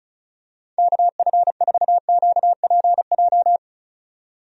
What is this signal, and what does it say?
Morse code, heard as a keyed tone, about 34 wpm, 700 Hz. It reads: KF4QPJ
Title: USA Call Signs - 6 Characters - 35wpm